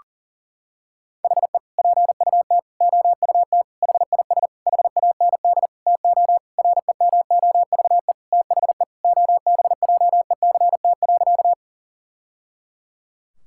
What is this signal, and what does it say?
Morse code, heard as a keyed tone, about 40 wpm, 700 Hz. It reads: HE PUT OUT HIS HAND TO REMOVE THE OBJECT.
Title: Sentences from Top 1000 Words - Review Part 5 - 40wpm